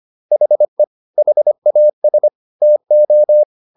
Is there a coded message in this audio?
Yes: Morse code from 0.3 to 3.4 s, heard as a keyed tone, about 25 wpm, 600 Hz. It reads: HE HAS TO